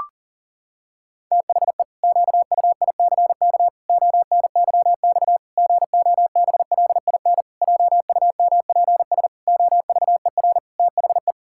Morse code, heard as a keyed tone, about 40 wpm, 700 Hz. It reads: THE QUICK ONYX GOBLIN JUMPS OVER THE
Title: Pangram Challenge v2 - 40wpm